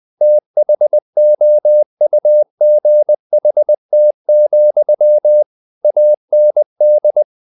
Morse code, heard as a keyed tone, about 20 wpm, 600 Hz. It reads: THOUGHT, AND
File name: the old man and the sea / the old man and the sea part 1 20wpm